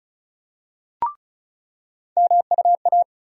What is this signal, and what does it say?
Morse code, heard as a keyed tone, about 34 wpm, 700 Hz. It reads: MUA
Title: USA Call Sign Postfixes - 3 Letters 35wpm